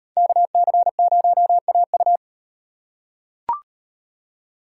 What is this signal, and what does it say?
Morse code, heard as a keyed tone, about 36 wpm, 700 Hz. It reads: KC0AU